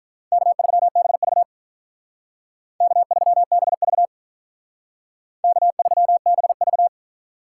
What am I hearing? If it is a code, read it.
Morse code, heard as a keyed tone, about 45 wpm, 700 Hz. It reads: K3BV K3BV K3BV